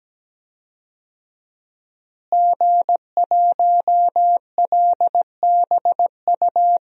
Morse code, heard as a keyed tone, about 17 wpm, 700 Hz. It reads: G1LBU
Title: DXCC-100 Call Signs - 5 Characters - Repeated 2x - 17wpm